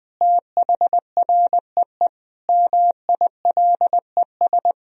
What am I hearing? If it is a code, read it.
Morse code, heard as a keyed tone, about 20 wpm, 700 Hz. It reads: THREE MILES
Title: Sentences from Top 1000 Words - Review Part 3 - 20wpm